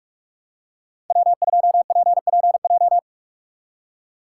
Morse code, heard as a keyed tone, about 45 wpm, 690 Hz. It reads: W2PPJ
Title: USA Call Signs - 4x Character Spacing - 45wpm